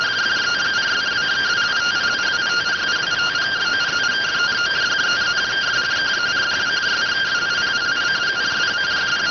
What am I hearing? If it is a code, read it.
Two-tone teleprinter signal, about 170 Hz shift.